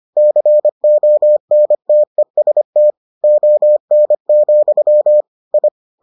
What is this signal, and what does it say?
Morse code, heard as a keyed tone, about 25 wpm, 600 Hz. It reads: CONTEST ON, I